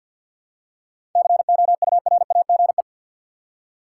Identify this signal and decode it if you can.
Morse code, heard as a keyed tone, about 52 wpm, 700 Hz. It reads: COURAGE
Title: Seven Letter Words - 50wpm